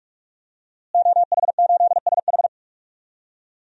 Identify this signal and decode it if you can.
Morse code, heard as a keyed tone, about 46 wpm, 700 Hz. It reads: OH8SH
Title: DXCC-100 Call Signs - 5 Characters - Repeated 3x - 45wpm